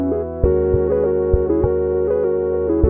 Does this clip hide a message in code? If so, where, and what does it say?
No.